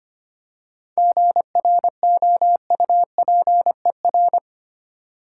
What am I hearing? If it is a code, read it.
Morse code, heard as a keyed tone, about 25 wpm, 700 Hz. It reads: GROUPER
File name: practice / 25wpm / Difficult Words with Hints - 25wpm